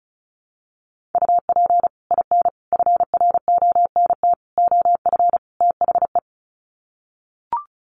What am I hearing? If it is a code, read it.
Morse code, heard as a keyed tone, about 34 wpm, 700 Hz. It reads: UP IN FRONT OF THE
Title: Sets of 5 Words Encore2 35wpm